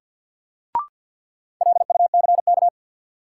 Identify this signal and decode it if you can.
Morse code, heard as a keyed tone, about 49 wpm, 700 Hz. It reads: LUCK
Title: Sentences from Top 1500 Words - New Word Drill - 50wpm